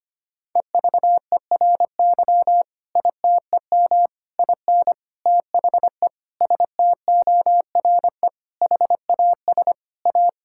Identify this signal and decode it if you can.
Morse code, heard as a keyed tone, about 25 wpm, 700 Hz. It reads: EVERY ITEM IN THE STORE HAS A
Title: Sentences from Top 1000 Words - Review Part 6 - 25wpm